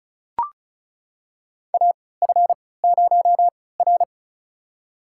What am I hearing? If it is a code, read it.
Morse code, heard as a keyed tone, about 36 wpm, spaced as for 20 wpm, 700 Hz. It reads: AF0R
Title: USA Call Signs - 4 Characters - 3x Character Spacing - 35wpm